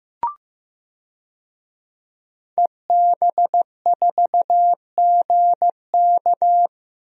Morse code, heard as a keyed tone, about 15 wpm, 700 Hz. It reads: EB4GK
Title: DXCC-100 Call Signs - 5 Characters - Repeated 2x - 15wpm